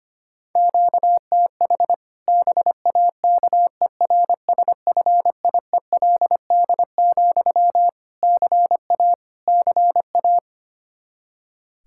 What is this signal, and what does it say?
Morse code, heard as a keyed tone, about 25 wpm, 700 Hz. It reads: QTH BAKERSFIELD, CA CA
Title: Practice QSO - Exchange 1 - Beginning - 25wpm